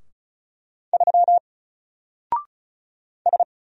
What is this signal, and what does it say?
Morse code, heard as a keyed tone, about 36 wpm, 700 Hz. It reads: 3 S
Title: Single Character (Letter:Number:Punctuation:Prosign) 35wpm